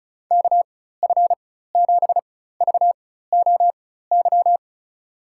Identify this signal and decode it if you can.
Morse code, heard as a keyed tone, about 34 wpm, spaced as for 16 wpm, 700 Hz. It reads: KF7VOY
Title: USA Call Signs - 6 Characters - 4x Character Spacing - 35wpm